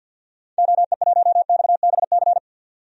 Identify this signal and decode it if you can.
Morse code, heard as a keyed tone, about 52 wpm, 700 Hz. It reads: OE1XBC
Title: DXCC-100 Call Signs - 6 Characters - 50wpm